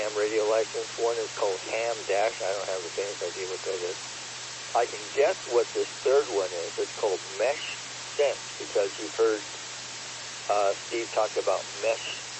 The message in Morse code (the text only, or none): none